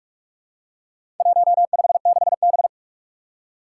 Morse code, heard as a keyed tone, about 46 wpm, 700 Hz. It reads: 156B
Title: Four Challenging Letter-Number Combinations 45wpm